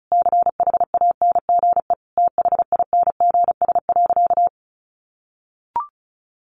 Morse code, heard as a keyed tone, about 34 wpm, 700 Hz. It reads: CHANGE THINGS.
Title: A Sound Of Thunder 35wpm